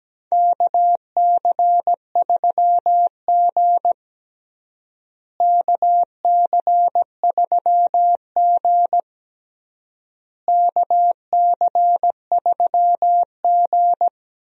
Morse code, heard as a keyed tone, about 17 wpm, 700 Hz. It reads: KC3G KC3G KC3G
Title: USA Call Signs - 4 Characters - Repeated 3x - 17wpm